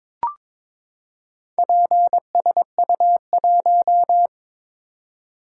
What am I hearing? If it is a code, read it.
Morse code, heard as a keyed tone, about 22 wpm, 700 Hz. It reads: PSU1
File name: practice / 22wpm / Four Letter-Number Combinations - 22wpm